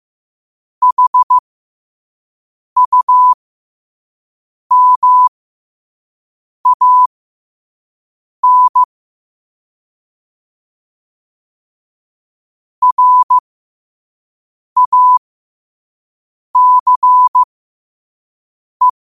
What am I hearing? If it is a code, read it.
Morse code, heard as a keyed tone, about 15 wpm, spaced as for 5 wpm, 1000 Hz. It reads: HUMAN RACE
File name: qotdfiles / 5wpm